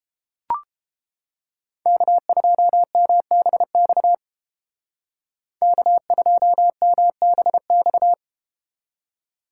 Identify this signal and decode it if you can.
Morse code, heard as a keyed tone, about 30 wpm, 700 Hz. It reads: K2MBX K2MBX